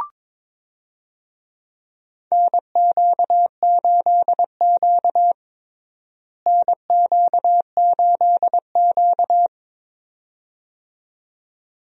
Morse code, heard as a keyed tone, about 22 wpm, 700 Hz. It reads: NQ8Q NQ8Q
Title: USA Call Signs - 4 Characters - Repeated 2x - 22wpm